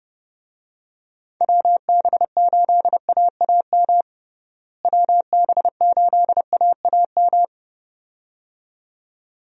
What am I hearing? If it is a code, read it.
Morse code, heard as a keyed tone, about 30 wpm, 700 Hz. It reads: WB8AAM WB8AAM
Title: USA Call Signs - 6 Characters - Repeated 2x - 30wpm